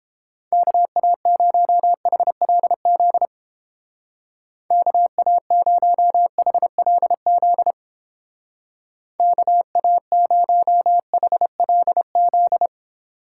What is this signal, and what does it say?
Morse code, heard as a keyed tone, about 31 wpm, 700 Hz. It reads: KA0HLZ KA0HLZ KA0HLZ